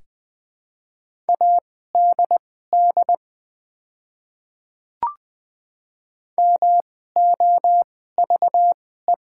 Morse code, heard as a keyed tone, about 20 wpm, spaced as for 14 wpm, 700 Hz. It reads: ADD MOVE
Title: 2x Character Spacing - Top 200 Words - 20wpm